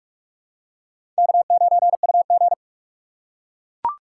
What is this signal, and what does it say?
Morse code, heard as a keyed tone, about 44 wpm, 700 Hz. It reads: K9UG
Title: USA Call Signs - 4 Characters - 2x Character Spacing - 45wpm